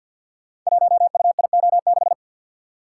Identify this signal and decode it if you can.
Morse code, heard as a keyed tone, about 52 wpm, 700 Hz. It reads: 1UIO6
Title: Code Groups - 5 Letter:Numbers - 50wpm